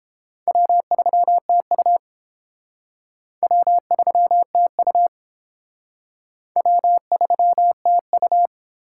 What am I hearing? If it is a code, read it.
Morse code, heard as a keyed tone, about 30 wpm, 700 Hz. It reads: W3TU W3TU W3TU